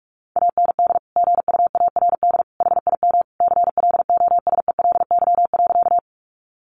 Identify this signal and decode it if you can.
Morse code, heard as a keyed tone, about 44 wpm, 700 Hz. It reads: AND GUARD HIM CLOSELY.
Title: Sentences from Top 1400 Words 45wpm